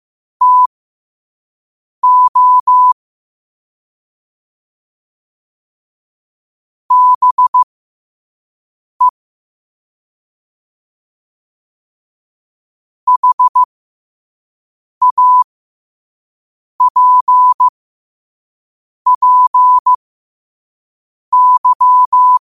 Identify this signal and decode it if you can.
Morse code, heard as a keyed tone, about 15 wpm, spaced as for 5 wpm, 1000 Hz. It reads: TO BE HAPPY
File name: qotdfiles / 5wpm